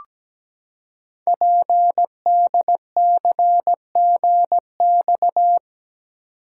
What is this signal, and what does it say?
Morse code, heard as a keyed tone, about 17 wpm, 700 Hz. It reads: PDCGX
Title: Code Groups - 5 Letters - 17wpm